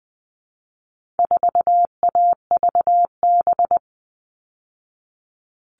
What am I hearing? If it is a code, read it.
Morse code, heard as a keyed tone, about 20 wpm, 700 Hz. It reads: 4AVB
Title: All Cut Numbers - 4-Digit 20wpm